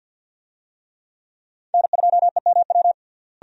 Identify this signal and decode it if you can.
Morse code, heard as a keyed tone, about 52 wpm, 700 Hz. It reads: N2EMW
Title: USA Call Signs - 5 Characters - Repeated 2x - 50wpm